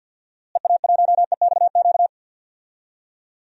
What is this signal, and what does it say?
Morse code, heard as a keyed tone, about 47 wpm, 700 Hz. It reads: EA1EXQ
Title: DXCC-100 Call Signs - 6 Characters - Repeated 2x - 50wpm